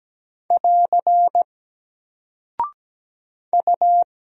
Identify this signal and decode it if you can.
Morse code, heard as a keyed tone, about 17 wpm, 700 Hz. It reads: <AR> U
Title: Single Character (Letter:Number:Punctuation:Prosign) 17wpm